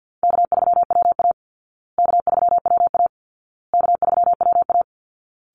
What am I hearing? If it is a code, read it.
Morse code, heard as a keyed tone, about 48 wpm, 700 Hz. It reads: K3WA K3WA K3WA
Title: Easy Call Signs Repeated 3x - 50wpm